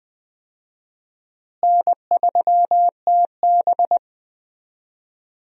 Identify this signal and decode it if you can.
Morse code, heard as a keyed tone, about 20 wpm, 700 Hz. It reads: N3TB
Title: USA Call Signs - 4 Characters - Repeated 3x - 20wpm